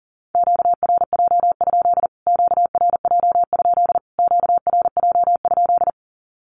Morse code, heard as a keyed tone, about 37 wpm, 700 Hz. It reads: QRJ? QRJ? QRJ?
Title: Q-Codes Repeated 3x 40wpm